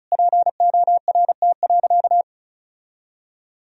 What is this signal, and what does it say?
Morse code, heard as a keyed tone, about 34 wpm, 700 Hz. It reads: PORT.
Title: Sentences from Top 1000 Words - Review Part 1 - 35wpm